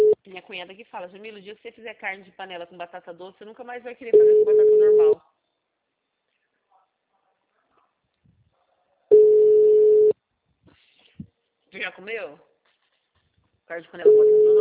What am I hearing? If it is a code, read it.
Ringback tone (Europe, 425 Hz), with speech.